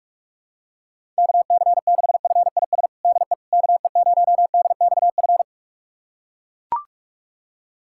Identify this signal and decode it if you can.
Morse code, heard as a keyed tone, about 42 wpm, 700 Hz. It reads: KC6FIS DE KE0DXF
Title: USA Callsign DE Callsign - 6 Characters - 45wpm